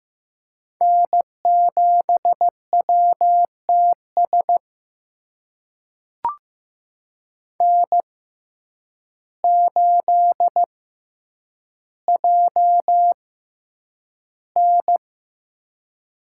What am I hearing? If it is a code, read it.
Morse code, heard as a keyed tone, about 15 wpm, spaced as for 8 wpm, 700 Hz. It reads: N7WTS N8JN
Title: USA Call Signs - 5 Characters - 6x Character Spacing - 15wpm